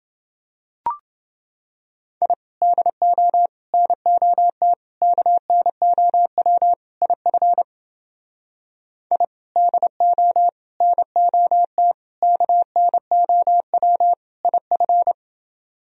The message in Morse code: I DO NOT KNOW IF I DO NOT KNOW IF

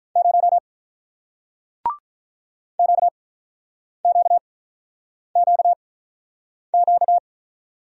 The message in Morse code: 0 Q Q Q Q

